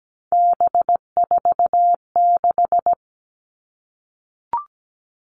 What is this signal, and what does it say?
Morse code, heard as a keyed tone, about 17 wpm, 700 Hz. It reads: B46
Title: All Cut Numbers - 3-Digit 17wpm